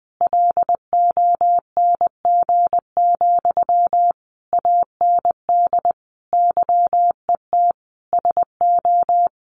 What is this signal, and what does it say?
Morse code, heard as a keyed tone, about 20 wpm, 700 Hz. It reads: LONG, AND YET SO
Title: Sentences from Top 500 Words 20wpm